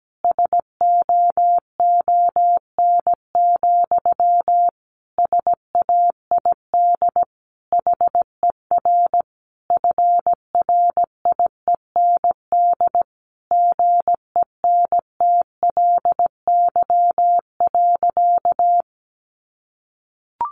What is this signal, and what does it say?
Morse code, heard as a keyed tone, about 17 wpm, 700 Hz. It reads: SOON, SAID HER FRIEND GENTLY.
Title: Sentences from Top 1500 Words 17wpm